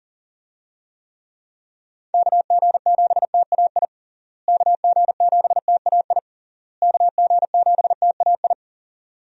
Morse code, heard as a keyed tone, about 40 wpm, 700 Hz. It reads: KG7TAI KG7TAI KG7TAI
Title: USA Call Signs - 6 Characters - Repeated 3x - 40wpm